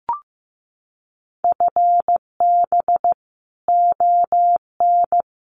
Morse code, heard as a keyed tone, about 15 wpm, 700 Hz. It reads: FB ON
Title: Common QSO Phrases Encore 15wpm